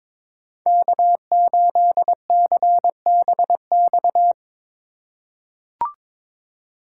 Morse code, heard as a keyed tone, about 22 wpm, 700 Hz. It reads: K8CBX
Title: USA Call Signs - 4x Character Spacing - 22wpm